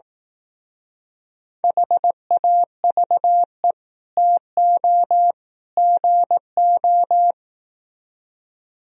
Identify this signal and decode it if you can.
Morse code, heard as a keyed tone, about 17 wpm, 700 Hz. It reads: HAVE TO GO
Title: Speed-Racing - Sets of 3 Words 15wpm